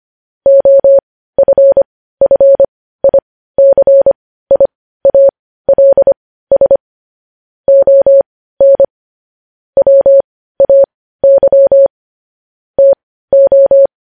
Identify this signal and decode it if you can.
Morse code, heard as a keyed tone, about 25 wpm, spaced as for 15 wpm, 550 Hz. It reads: OFFICIALS ON WAY TO
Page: news-headlines-in-morse-code-at-15-wpm